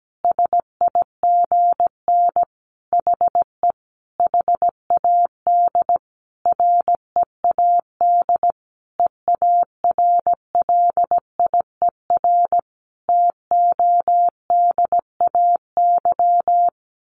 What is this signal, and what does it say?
Morse code, heard as a keyed tone, about 17 wpm, 700 Hz. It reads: SIGN HE HAD READ EARLIER TODAY
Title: A Sound Of Thunder 17wpm